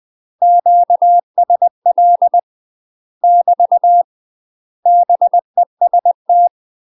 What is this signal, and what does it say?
Morse code, heard as a keyed tone, about 20 wpm, 700 Hz. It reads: QSL <BT> BEST
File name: UK MASTER QSOs TRAINING / 20WPM / UK MASTER QSOs 0005